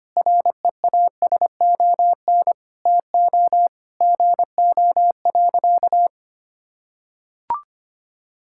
Morse code, heard as a keyed tone, about 25 wpm, 700 Hz. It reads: REASON TO GO.
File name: practice / 25wpm / Sentences with 3-Word Phrases - 25wpm